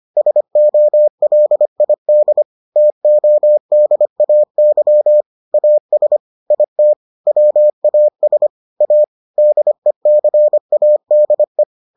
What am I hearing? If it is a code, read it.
Morse code, heard as a keyed tone, about 25 wpm, 600 Hz. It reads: SOLID TODAY AS IT WAS A DECADE